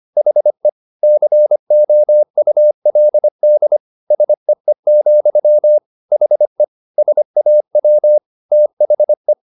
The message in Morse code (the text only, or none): HE COULD SEE, HE SAW THE